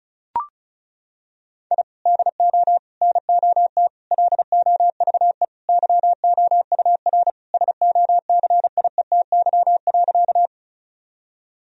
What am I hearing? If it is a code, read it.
Morse code, heard as a keyed tone, about 34 wpm, 700 Hz. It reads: I DO NOT LOVE YOUR SOCIETY.
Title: Sentences from Top 1000 Words - Review Part 4 - 35wpm